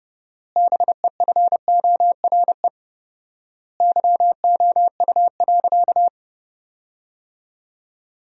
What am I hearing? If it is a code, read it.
Morse code, heard as a keyed tone, about 30 wpm, 700 Hz. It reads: BEFORE YOU.
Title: 4x Word-Spacing - Sentences from Top 200 Words - 30wpm